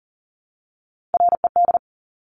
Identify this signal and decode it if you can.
Morse code, heard as a keyed tone, about 41 wpm, 700 Hz. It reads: RED
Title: Top Three Letter Combinations in Words 40wpm